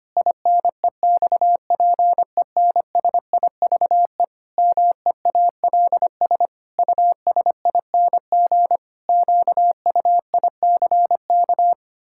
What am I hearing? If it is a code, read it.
Morse code, heard as a keyed tone, about 25 wpm, 700 Hz. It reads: INEXPENSIVE MEALS USING QUICK
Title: Pangram Challenge v2 - 25wpm